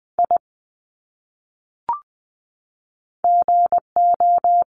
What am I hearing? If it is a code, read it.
Morse code, heard as a keyed tone, about 20 wpm, 700 Hz. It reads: I GO
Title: Top 300 Words 20wpm